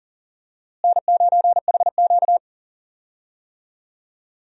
Morse code, heard as a keyed tone, about 39 wpm, 700 Hz. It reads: N9HQ
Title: USA Call Signs - 4 Characters - Repeated 3x - 40wpm